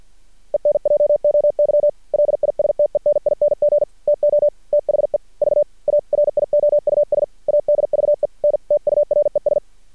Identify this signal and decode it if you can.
Morse code, heard as a keyed tone, about 48 wpm, 590 Hz. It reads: ENJOY LISTENING TO THE VARIOUS ADVENTURES
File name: archived / CRSnet